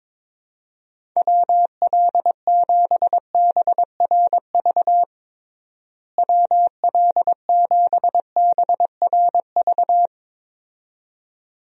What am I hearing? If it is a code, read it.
Morse code, heard as a keyed tone, about 22 wpm, 700 Hz. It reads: WL7BRV WL7BRV
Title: USA Call Signs - 6 Characters - Repeated 2x - 22wpm